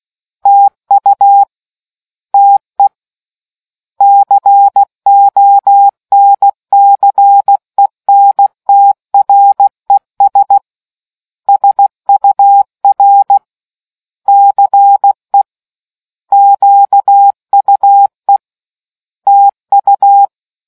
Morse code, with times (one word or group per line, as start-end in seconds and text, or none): 0.5-1.4 TU
2.3-2.9 TE
4.0-10.6 CONCENTRES
11.5-13.4 SUR
14.3-15.4 CE
16.3-18.4 QUE
19.3-20.3 TU